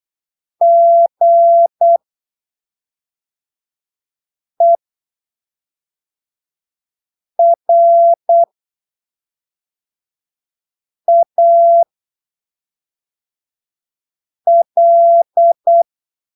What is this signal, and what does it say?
Morse code, heard as a keyed tone, about 8 wpm, spaced as for 3 wpm, 680 Hz. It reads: GERAL
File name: arquivos / texto / texto 14 08ppm